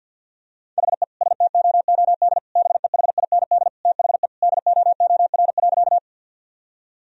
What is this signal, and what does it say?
Morse code, heard as a keyed tone, about 50 wpm, 700 Hz. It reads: HE STOOD BEHIND THE DOOR.
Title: Sentences from Top 1000 Words - Review Part 2 - 50wpm